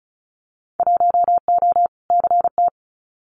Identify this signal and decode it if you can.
Morse code, heard as a keyed tone, about 34 wpm, 700 Hz. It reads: 1O CT
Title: Winter Field Day Exchanges 35wpm